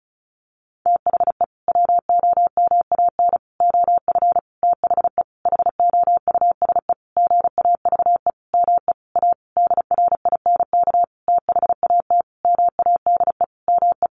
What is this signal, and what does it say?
Morse code, heard as a keyed tone, about 34 wpm, 700 Hz. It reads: THE WOMAN OF THE HOUSE GAVE ME A DRINK THAT MADE ME
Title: Sentences from Top 1500 Words 35wpm